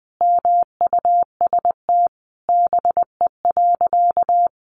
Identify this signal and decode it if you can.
Morse code, heard as a keyed tone, about 20 wpm, 700 Hz. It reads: MUST BE.
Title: Sentences from Top 700 Words 20wpm